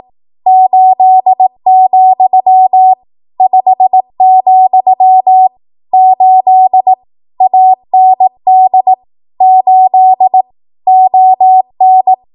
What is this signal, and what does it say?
Morse code, heard as a keyed tone, about 18 wpm, 750 Hz. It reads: 8, 5, 8 AND 8 ON
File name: Archive / 18 WPM / 18WPM